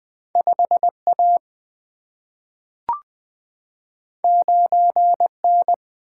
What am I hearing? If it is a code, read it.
Morse code, heard as a keyed tone, about 20 wpm, 700 Hz. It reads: 5 A 9 N
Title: Single-Letter Single-Number 20wpm